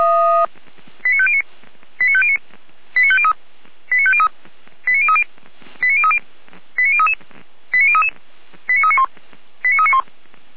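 Five-tone selective call (ZVEI), heard as quick runs of five tones, tones of about 70 ms. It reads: 89509 89509 89503 89503 89030 89030 89033 89033 89391 89391
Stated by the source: ZVEI five-tone call received by SDR, from the feuerwehr-orlamuende.org fire brigade audio server, station f85360.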